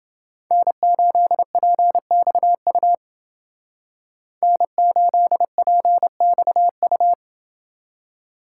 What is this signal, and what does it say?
Morse code, heard as a keyed tone, about 29 wpm, 700 Hz. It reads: N8PXU N8PXU